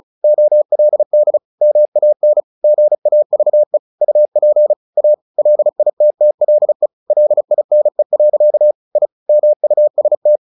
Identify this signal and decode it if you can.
Morse code, heard as a keyed tone, about 34 wpm, 600 Hz. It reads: OLD MAN GAVE UP A LITTLE LINE. I MUST
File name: the old man and the sea 2 35wpm